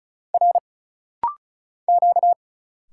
Morse code, heard as a keyed tone, about 34 wpm, 700 Hz. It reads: R Q
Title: Single Letter-Number - Rapid-Fire 35wpm